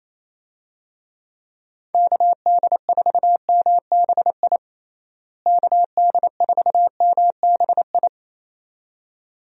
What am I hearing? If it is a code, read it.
Morse code, heard as a keyed tone, about 28 wpm, 700 Hz. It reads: KD4MBI KD4MBI
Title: USA Call Signs - 6 Characters - Repeated 2x - 28wpm